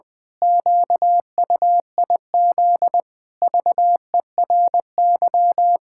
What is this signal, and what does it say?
Morse code, heard as a keyed tone, about 20 wpm, 700 Hz. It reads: QUIZ VERY
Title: Pangram Challenge v2 - 20wpm